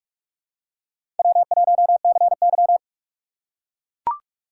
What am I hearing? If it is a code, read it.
Morse code, heard as a keyed tone, about 42 wpm, 700 Hz. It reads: W1CY
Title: USA Call Signs - 4 Characters - 4x Character Spacing - 45wpm